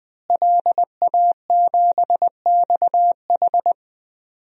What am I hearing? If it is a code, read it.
Morse code, heard as a keyed tone, about 20 wpm, 700 Hz. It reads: LA7XH
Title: DXCC-100 Call Signs - 5 Characters - Repeated 3x - 20wpm